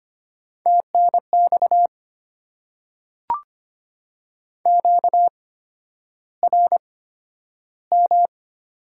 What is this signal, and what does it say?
Morse code, heard as a keyed tone, about 25 wpm, spaced as for 11 wpm, 700 Hz. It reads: TNX QRM
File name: Top 90 QSO Elements - 8x Character Spacing - 25wpm